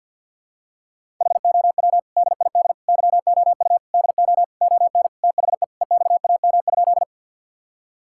Morse code, heard as a keyed tone, about 51 wpm, 700 Hz. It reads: HOW DID YOU DO ON THE EXAM?
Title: Sentences from Top 1000 Words - Review Part 5 - 50wpm